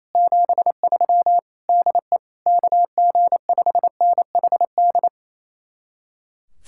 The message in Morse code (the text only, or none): 73 DE KG5NHD